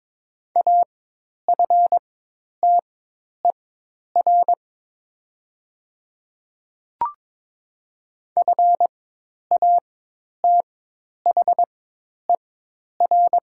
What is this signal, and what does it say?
Morse code, heard as a keyed tone, about 22 wpm, spaced as for 10 wpm, 700 Hz. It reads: AFTER FATHER
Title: 4x Character Spacing - Top 200 Words - 22wpm